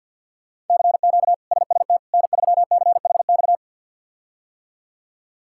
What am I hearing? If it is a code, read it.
Morse code, heard as a keyed tone, about 51 wpm, 700 Hz. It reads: CQ SST N3CHX
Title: K1USN SST QSO Practice - 50wpm